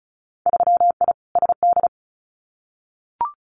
Morse code, heard as a keyed tone, about 36 wpm, 700 Hz. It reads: 3I SD
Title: Winter Field Day Exchanges Encore 35wpm